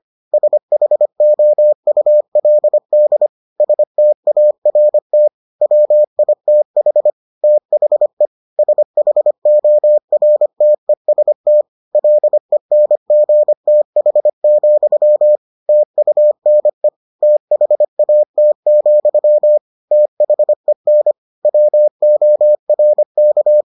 Morse code, heard as a keyed tone, about 25 wpm, 600 Hz. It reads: SHOULD START WITH THE SHORTEST LENGTH, TUNE THAT, THEN WORK